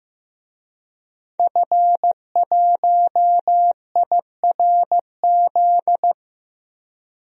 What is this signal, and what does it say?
Morse code, heard as a keyed tone, about 16 wpm, 700 Hz. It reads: F1IRZ
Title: DXCC-100 Call Signs - 5 Characters - Repeated 3x - 15wpm